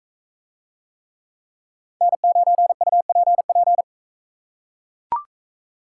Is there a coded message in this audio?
Yes: Morse code from 2.0 to 3.8 s, heard as a keyed tone, about 42 wpm, 700 Hz. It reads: N9UPP